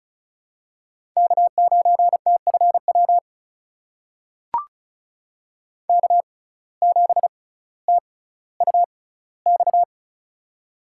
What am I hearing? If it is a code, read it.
Morse code, heard as a keyed tone, about 36 wpm, spaced as for 18 wpm, 700 Hz. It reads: K9TFW K7TUX
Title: USA Call Signs - 5 Characters - 6x Character Spacing - 35wpm